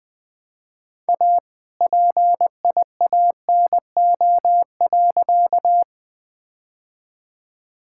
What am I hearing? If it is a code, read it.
Morse code, heard as a keyed tone, about 21 wpm, 700 Hz. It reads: A PIANO.
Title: All-Age Appropriate Jokes - 20wpm